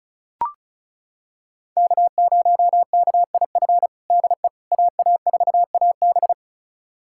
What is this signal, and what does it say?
Morse code, heard as a keyed tone, about 36 wpm, 700 Hz. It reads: K0KIF DE AA4AB
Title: USA Callsign DE Callsign - 5 Characters - 35wpm